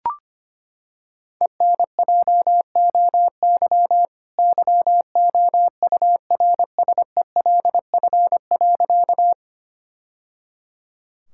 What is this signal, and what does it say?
Morse code, heard as a keyed tone, about 25 wpm, 700 Hz. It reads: ENJOY YOURSELF.
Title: Sentences from Top 1000 Words - Review Part 5 - 25wpm